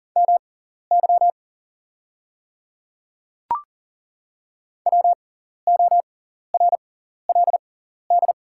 Morse code, heard as a keyed tone, about 40 wpm, spaced as for 14 wpm, 700 Hz. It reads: MY WORLD
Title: 6x Character Spacing - Top 200 Words - 40wpm